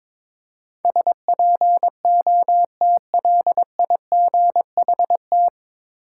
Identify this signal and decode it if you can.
Morse code, heard as a keyed tone, about 22 wpm, 700 Hz. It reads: SPOTLIGHT
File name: 9 Letter Words - First 3 Letters Given - 22wpm